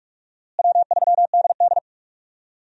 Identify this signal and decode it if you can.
Morse code, heard as a keyed tone, about 43 wpm, 700 Hz. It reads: W3DD
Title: USA Call Signs - 4 Characters - Repeated 2x - 45wpm